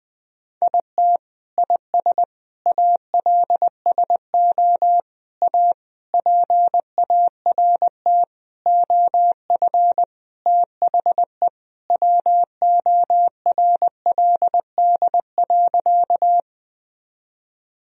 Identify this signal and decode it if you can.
Morse code, heard as a keyed tone, about 20 wpm, 700 Hz. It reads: IT IS ALSO A PART OF THE WORLD.
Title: Sentences from Top 1000 Words - Review Part 1 - 20wpm